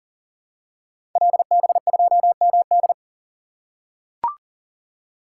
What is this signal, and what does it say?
Morse code, heard as a keyed tone, about 38 wpm, 700 Hz. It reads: LB2MD
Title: Code Groups - 5 Letter:Numbers - 40wpm